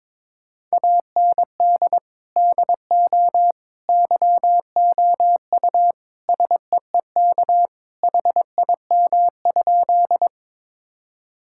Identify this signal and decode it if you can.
Morse code, heard as a keyed tone, about 22 wpm, 700 Hz. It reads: AND DO YOU SEEK HIM?
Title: Sentences from Top 1600 Words 22wpm